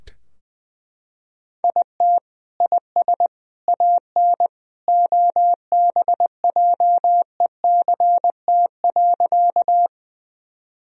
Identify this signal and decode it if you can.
Morse code, heard as a keyed tone, about 20 wpm, 700 Hz. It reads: IT IS AN OBJECT.
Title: Sentences from Top 1000 Words - Review Part 3 - 20wpm